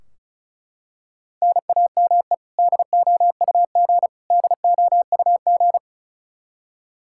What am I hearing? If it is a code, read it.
Morse code, heard as a keyed tone, about 34 wpm, 700 Hz. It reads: NAME DOUG DOUG
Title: Practice QSO - Exchange 1 - Beginning - 35wpm